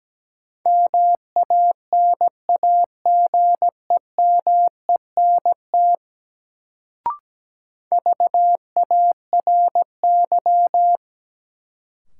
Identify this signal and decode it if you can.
Morse code, heard as a keyed tone, about 17 wpm, 700 Hz. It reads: MANAGEMENT VARY
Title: Sentences from Top 1200 Words - New Word Drill Alt - 17wpm